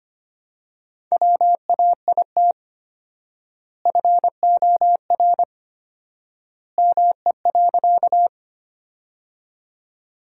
Morse code, heard as a keyed tone, about 25 wpm, 700 Hz. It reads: WAIT FOR ME.
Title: 4x Word-Spacing - Sentences from Top 500 Words - 25wpm